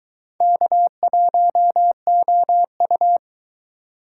Morse code, heard as a keyed tone, about 23 wpm, 700 Hz. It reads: K1OU